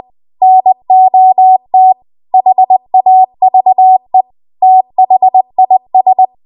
Morse code, heard as a keyed tone, about 20 wpm, 750 Hz. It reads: NOT HAVE THIS
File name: Archive / 20 WPM / 20WPM